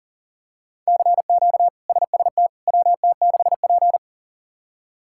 Morse code, heard as a keyed tone, about 40 wpm, 700 Hz. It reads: CQ SST WT6P
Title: K1USN SST QSO Practice - 40wpm